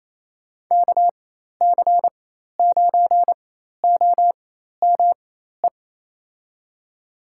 Morse code, heard as a keyed tone, about 28 wpm, spaced as for 13 wpm, 700 Hz. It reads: KC9OME
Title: USA Call Signs - 6 Characters - 4x Character Spacing - 28wpm